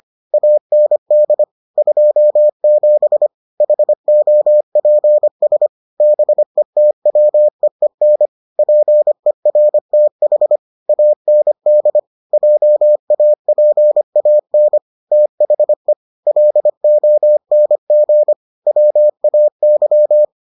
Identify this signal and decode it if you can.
Morse code, heard as a keyed tone, about 25 wpm, 600 Hz. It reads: AND 27 HOPS BETWEEN PERTH AND JAPAN THE LONG WAY